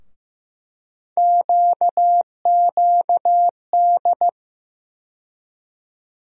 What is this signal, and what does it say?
Morse code, heard as a keyed tone, about 15 wpm, 700 Hz. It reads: QQD
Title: USA Call Sign Postfixes - 3 Letters 15wpm